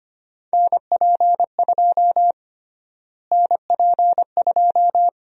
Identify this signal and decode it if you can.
Morse code, heard as a keyed tone, about 25 wpm, 700 Hz. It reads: NP2 NP2
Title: USA Call Sign Prefixes - Repeated 2x - 25wpm